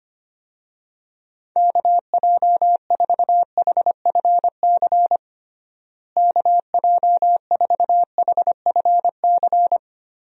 Morse code, heard as a keyed tone, about 25 wpm, 700 Hz. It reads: KJ4HFC KJ4HFC
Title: USA Call Signs - 6 Characters - Repeated 3x - 25wpm